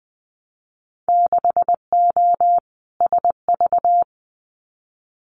Morse code, heard as a keyed tone, about 19 wpm, 700 Hz. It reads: 6O SV
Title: Winter Field Day Exchanges Encore 20wpm